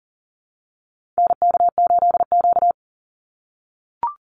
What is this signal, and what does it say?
Morse code, heard as a keyed tone, about 40 wpm, 700 Hz. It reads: NK8Q
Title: NA SOTA Activator Callsigns 40wpm